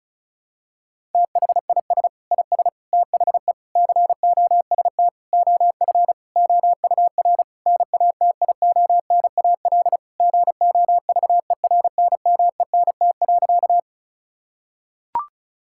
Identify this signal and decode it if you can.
Morse code, heard as a keyed tone, about 34 wpm, 700 Hz. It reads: THIS IS THE COST OF OUR NATIONAL GOVERNMENT.
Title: Sentences from Top 1000 Words - Review Part 5 - 35wpm